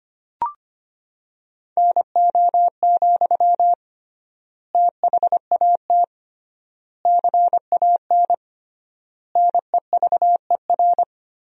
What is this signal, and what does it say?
Morse code, heard as a keyed tone, about 25 wpm, 700 Hz. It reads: NO, THAT CAN NEVER
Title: 3x Word-Spacing - Sentences from Top 300 Words - 25wpm